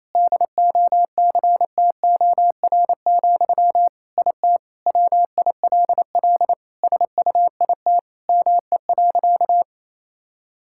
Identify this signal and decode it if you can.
Morse code, heard as a keyed tone, about 28 wpm, 700 Hz. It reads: DOCTOR, IT WILL SUIT ME.
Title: Sentences from Top 1600 Words 28wpm